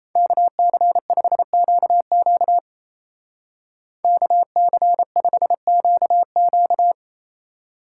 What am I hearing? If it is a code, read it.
Morse code, heard as a keyed tone, about 34 wpm, 700 Hz. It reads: KC5QQ KC5QQ
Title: Speed Racing - USA 2x2 Call Signs 28wpm